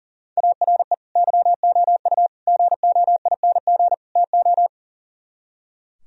Morse code, heard as a keyed tone, about 36 wpm, 700 Hz. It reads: ARE YOU GOING TO
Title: Speed-Racing - Sets of 4 Words 40wpm